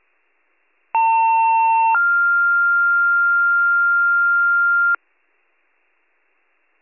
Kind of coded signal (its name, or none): two-tone page